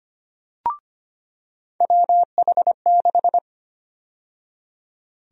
Morse code, heard as a keyed tone, about 25 wpm, 700 Hz. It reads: WH6
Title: USA Call Sign Prefixes - 25wpm